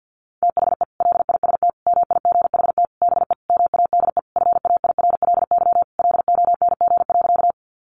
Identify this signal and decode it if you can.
Morse code, heard as a keyed tone, about 50 wpm, 700 Hz. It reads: THE LIST MIGHT BE MADE FAIRLY LONG.
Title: Sentences from Top 1100 Words 50wpm